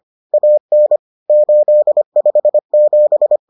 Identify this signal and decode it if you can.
Morse code, heard as a keyed tone, about 25 wpm, 600 Hz. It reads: AN 857